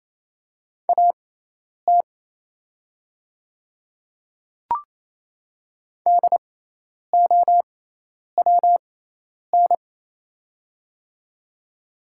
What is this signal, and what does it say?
Morse code, heard as a keyed tone, about 28 wpm, spaced as for 10 wpm, 700 Hz. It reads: AT DOWN
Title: 6x Character Spacing - Top 200 Words - 28wpm